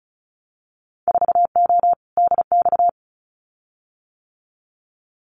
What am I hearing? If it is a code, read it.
Morse code, heard as a keyed tone, about 34 wpm, 700 Hz. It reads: 4O DX
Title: Winter Field Day Exchanges Encore 35wpm